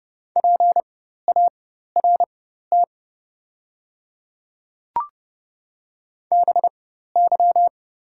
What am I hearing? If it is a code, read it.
Morse code, heard as a keyed tone, about 30 wpm, spaced as for 14 wpm, 700 Hz. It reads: PART BY
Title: 4x Character Spacing - Top 200 Words - 30wpm